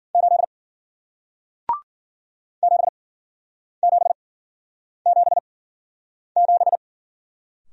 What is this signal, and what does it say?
Morse code, heard as a keyed tone, about 55 wpm, spaced as for 9 wpm, 700 Hz. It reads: 8 7777